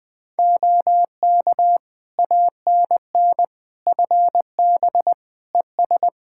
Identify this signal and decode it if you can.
Morse code, heard as a keyed tone, about 21 wpm, 700 Hz. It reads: OK ANN FB ES
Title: K1USN SST QSO Practice 20wpm